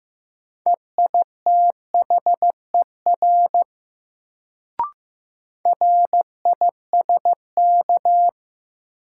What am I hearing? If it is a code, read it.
Morse code, heard as a keyed tone, about 16 wpm, 700 Hz. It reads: EITHER RISK
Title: Sentences from Top 600 Words - New Word Drill - 15wpm